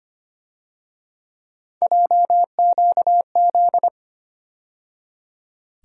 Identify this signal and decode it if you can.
Morse code, heard as a keyed tone, about 31 wpm, 700 Hz. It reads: JQZ
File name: Speed-Racing - Three Letter-Number Combinations - 25wpm